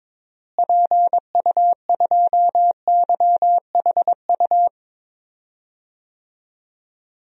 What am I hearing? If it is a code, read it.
Morse code, heard as a keyed tone, about 22 wpm, 700 Hz. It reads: PU2YHU
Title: DXCC-100 Call Signs - 6 Characters - Repeated 2x - 22wpm